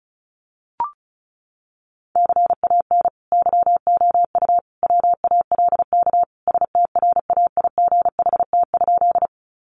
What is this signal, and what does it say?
Morse code, heard as a keyed tone, about 34 wpm, 700 Hz. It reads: CAN YOU WALK STRAIGHT?
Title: Sentences from Top 1000 Words 35wpm